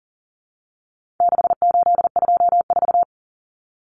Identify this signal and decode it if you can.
Morse code, heard as a keyed tone, about 40 wpm, 700 Hz. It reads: 6824
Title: Common Cut Numbers - 4-Digit 40wpm